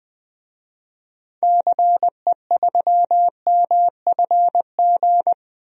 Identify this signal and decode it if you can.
Morse code, heard as a keyed tone, about 21 wpm, 700 Hz. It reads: CE3MFG
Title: DXCC-100 Call Signs - 6 Characters - 20wpm